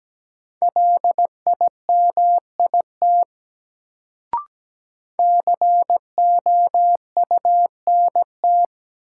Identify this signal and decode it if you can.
Morse code, heard as a keyed tone, about 17 wpm, 700 Hz. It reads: LIMIT COUNT
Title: Sentences from Top 1500 Words - New Word Drill Alt - 17wpm